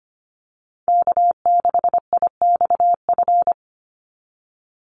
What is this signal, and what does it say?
Morse code, heard as a keyed tone, about 25 wpm, 700 Hz. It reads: K6IXF
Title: USA Call Signs 25wpm